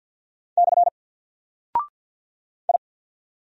Morse code, heard as a keyed tone, about 49 wpm, 700 Hz. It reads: / I
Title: Single Character (Letter:Number:Punctuation:Prosign) Alt 50wpm